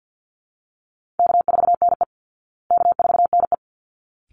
Morse code, heard as a keyed tone, about 48 wpm, 700 Hz. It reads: K4NE K4NE
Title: Call Signs Repeated 3x - 50wpm